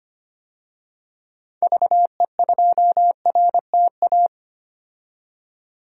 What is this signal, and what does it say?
Morse code, heard as a keyed tone, about 25 wpm, 700 Hz. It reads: VE2RTA
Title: DXCC-100 Call Signs - 6 Characters - 25wpm